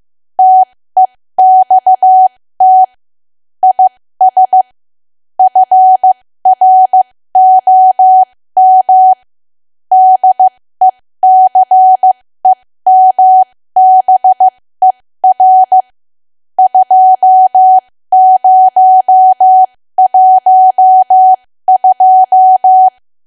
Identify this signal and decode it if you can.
Morse code, heard as a keyed tone, about 15 wpm, spaced as for 13 wpm, 750 Hz. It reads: TEXT IS FROM DECEMBER 2012
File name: cwa / W1AW-Level2 / 13WPM